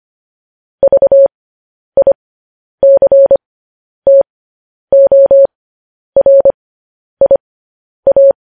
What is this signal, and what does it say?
Morse code, heard as a keyed tone, about 25 wpm, spaced as for 10 wpm, 550 Hz. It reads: VICTORIA
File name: podcast / News headlines-1x-25x10